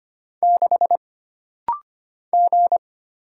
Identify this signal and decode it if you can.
Morse code, heard as a keyed tone, about 25 wpm, 700 Hz. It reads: 6 G
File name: Single Character (Letter:Number:Punctuation:Prosign) - Rapid-Fire - 25wpm